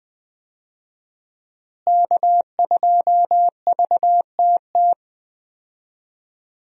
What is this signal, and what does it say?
Morse code, heard as a keyed tone, about 20 wpm, 700 Hz. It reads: K2VTT
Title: USA Call Signs - 5 Characters - Repeated 2x - 20wpm